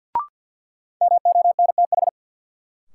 Morse code, heard as a keyed tone, about 53 wpm, 700 Hz. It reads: MONTH
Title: Sets of 5 Words Encore - New Word Drill - 50wpm